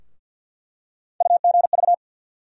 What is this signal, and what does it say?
Morse code, heard as a keyed tone, about 50 wpm, 700 Hz. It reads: UGV
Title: USA Call Sign Postfixes - 3 Letters 50wpm